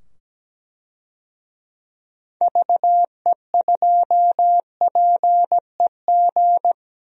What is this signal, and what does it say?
Morse code, heard as a keyed tone, about 17 wpm, 700 Hz. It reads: VE2PEG
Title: DXCC-100 Call Signs - 6 Characters - 17wpm